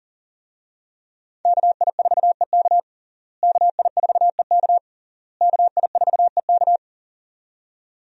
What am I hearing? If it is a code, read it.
Morse code, heard as a keyed tone, about 40 wpm, 700 Hz. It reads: KI4EK KI4EK KI4EK
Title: USA Call Signs - 5 Characters - Repeated 3x - 40wpm